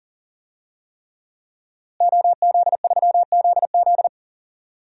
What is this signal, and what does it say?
Morse code, heard as a keyed tone, about 40 wpm, 700 Hz. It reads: OZ3ZZ
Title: DXCC-100 Call Signs - 5 Characters - Repeated 3x - 40wpm